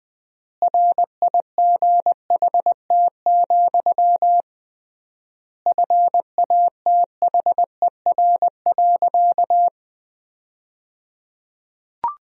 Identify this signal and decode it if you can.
Morse code, heard as a keyed tone, about 20 wpm, 700 Hz. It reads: RIGHT, FATHER.
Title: 3x Word-Spacing - Sentences from Top 300 Words - 20wpm